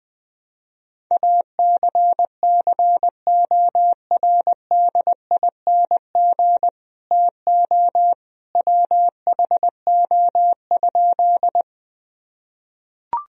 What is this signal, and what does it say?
Morse code, heard as a keyed tone, about 20 wpm, 700 Hz. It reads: ACCORDING TO WHO?
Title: Sentences from Top 1000 Words - Review Part 4 - 20wpm